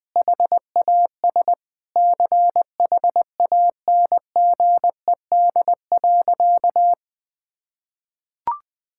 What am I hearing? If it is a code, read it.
Morse code, heard as a keyed tone, about 20 wpm, 700 Hz. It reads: HAS CHANGED.
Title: Sentences with 3-Word Phrases - Spoken Beginning - 20wpm